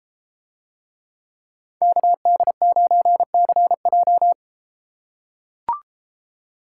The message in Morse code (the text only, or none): KD9CJ